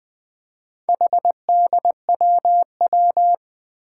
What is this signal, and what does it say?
Morse code, heard as a keyed tone, about 21 wpm, 700 Hz. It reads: HDWW
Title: Four Challenging Letter-Number Combinations 20wpm